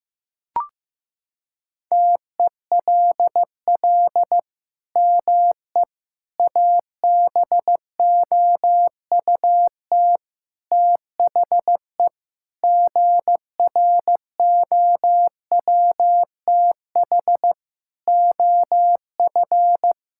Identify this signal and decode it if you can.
Morse code, heard as a keyed tone, about 15 wpm, 700 Hz. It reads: TELL ME ABOUT THE GROWTH OF
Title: Sentences from Top 1000 Words - Review Part 5 - 15wpm